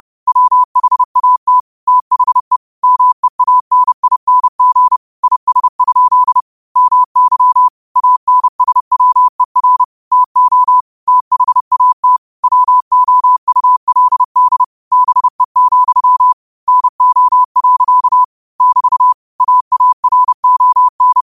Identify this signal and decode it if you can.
Morse code, heard as a keyed tone, about 30 wpm, 1000 Hz. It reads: WHAT THE MEANING IS? MY ANSWER TO THAT WOULD BE, NO. <BT> AARON
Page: New quotes every day in morse code at 30 Words per minute.